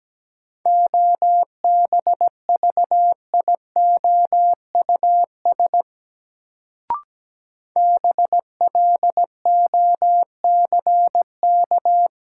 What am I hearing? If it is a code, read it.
Morse code, heard as a keyed tone, about 17 wpm, 700 Hz. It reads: OBVIOUS BLOCK
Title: Sentences from Top 1600 Words - New Word Drill - 17wpm